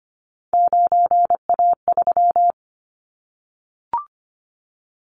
Morse code, heard as a keyed tone, about 25 wpm, 700 Hz. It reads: 9A3
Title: Callsign Prefixes for Top 25 DXCC - 25wpm